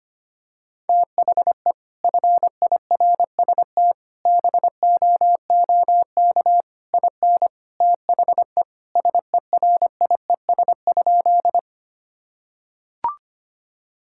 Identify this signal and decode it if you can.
Morse code, heard as a keyed tone, about 25 wpm, 700 Hz. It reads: THE FIRST BOOK IN THE SERIES?
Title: Sentences from Top 1000 Words - Review Part 5 - 25wpm